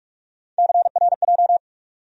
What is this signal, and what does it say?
Morse code, heard as a keyed tone, about 43 wpm, 700 Hz. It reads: CRJ
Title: USA Call Sign Postfixes - 3 Letters 45wpm